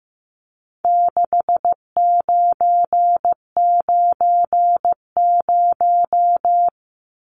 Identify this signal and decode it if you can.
Morse code, heard as a keyed tone, about 15 wpm, 700 Hz. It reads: 6990
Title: Numbers - 4-Digit 15wpm